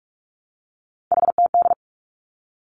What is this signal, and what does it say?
Morse code, heard as a keyed tone, about 46 wpm, 700 Hz. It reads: HTD
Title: Three Challenging Letter-Number Combinations 45wpm